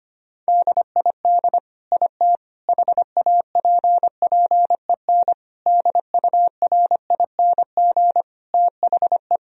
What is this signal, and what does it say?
Morse code, heard as a keyed tone, about 25 wpm, 700 Hz. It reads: DID IT HAPPEN DURING THE
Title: Sentences from Top 1000 Words - Review Part 2 - 25wpm